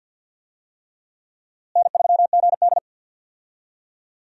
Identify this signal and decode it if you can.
Morse code, heard as a keyed tone, about 47 wpm, 700 Hz. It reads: N3GD
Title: USA Call Signs - 4 Characters - Repeated 3x - 50wpm